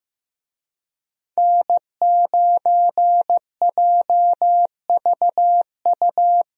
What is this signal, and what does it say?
Morse code, heard as a keyed tone, about 15 wpm, 700 Hz. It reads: N9JVU
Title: USA Call Signs - 5 Characters - Repeated 2x - 15wpm